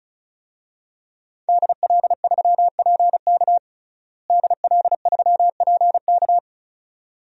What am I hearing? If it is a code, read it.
Morse code, heard as a keyed tone, about 34 wpm, 700 Hz. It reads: DL3PK DL3PK
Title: DXCC-100 Call Signs - 5 Characters - Repeated 2x - 35wpm